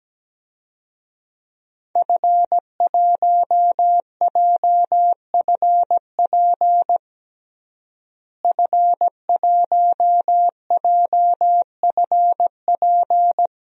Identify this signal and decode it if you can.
Morse code, heard as a keyed tone, about 17 wpm, 700 Hz. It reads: F1JFP F1JFP
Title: DXCC-100 Call Signs - 5 Characters - Repeated 2x - 17wpm